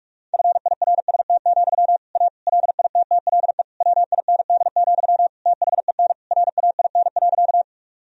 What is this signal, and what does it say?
Morse code, heard as a keyed tone, about 46 wpm, 700 Hz. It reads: FIRST, A LITTLE WIND, THEN RAIN.
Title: Sentences from Top 1000 Words - Review Part 2 - 45wpm